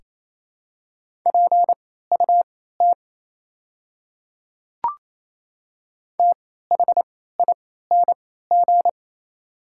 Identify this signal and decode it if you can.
Morse code, heard as a keyed tone, about 28 wpm, spaced as for 16 wpm, 700 Hz. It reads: PUT THING
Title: 3x Character Spacing - Top 300 Words - 28wpm